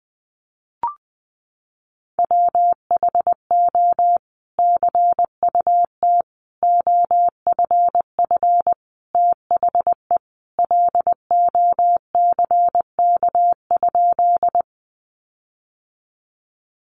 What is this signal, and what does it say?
Morse code, heard as a keyed tone, about 19 wpm, 700 Hz. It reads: WHO CUT OFF THE LOCK?
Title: Sentences from Top 1300 Words 20wpm